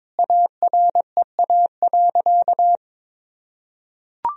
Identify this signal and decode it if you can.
Morse code, heard as a keyed tone, about 22 wpm, 700 Hz. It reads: AREA.
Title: Sentences from Top 1000 Words - Review Part 2 - 22wpm